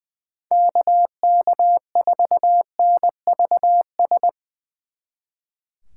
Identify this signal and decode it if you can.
Morse code, heard as a keyed tone, about 22 wpm, 700 Hz. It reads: KK4NVS
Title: Speed Racing - USA Call Signs 20wpm